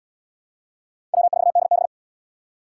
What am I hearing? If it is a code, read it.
Morse code, heard as a keyed tone, about 72 wpm, 700 Hz. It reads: W4DB